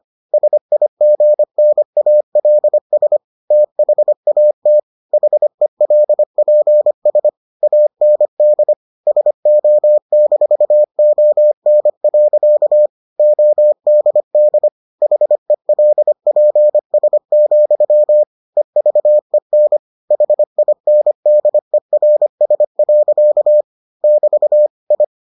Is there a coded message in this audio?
Yes: Morse code from 0.3 to 25.0 s, heard as a keyed tone, about 25 wpm, 600 Hz. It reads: SIGNALS THAT HELPS AND SO-ON. ODD HELPS, EVEN HINDERS. <BT> I